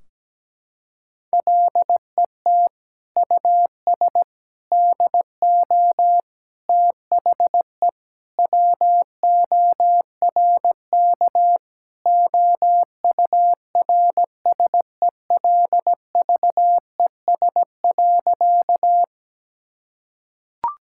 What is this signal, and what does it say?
Morse code, heard as a keyed tone, about 17 wpm, 700 Hz. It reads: LET US DO THE WORK OURSELVES.
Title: Sentences from Top 1600 Words 17wpm